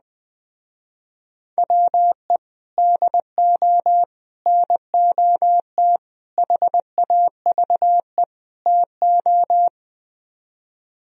Morse code, heard as a keyed tone, about 20 wpm, 700 Hz. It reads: WE DO NOT HAVE TO